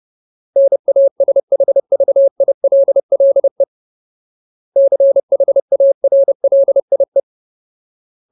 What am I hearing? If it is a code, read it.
Morse code, heard as a keyed tone, about 30 wpm, 550 Hz. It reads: NASHVILLE CHARLIE
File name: Ultimate Word List / 30WPM Ultimate